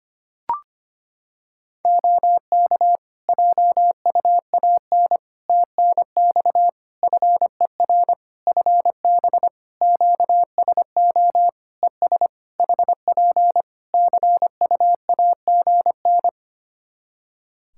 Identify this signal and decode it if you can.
Morse code, heard as a keyed tone, about 25 wpm, 700 Hz. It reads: OK JUAN TNX FER FB QSO ES HP CU AGN
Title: Practice QSO - Exchange 4 - Ending - 25wpm